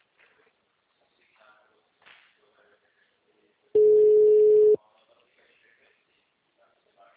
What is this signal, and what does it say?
Ringback tone (Europe, 425 Hz), with speech.